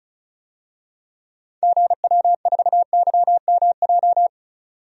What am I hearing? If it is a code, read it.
Morse code, heard as a keyed tone, about 34 wpm, 700 Hz. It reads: GW4YMJ
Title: DXCC-100 Call Signs - 6 Characters - 35wpm